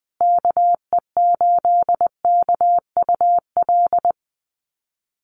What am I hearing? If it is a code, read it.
Morse code, heard as a keyed tone, about 20 wpm, 700 Hz. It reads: KE8KUL
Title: USA Call Signs Repeated 3x 20wpm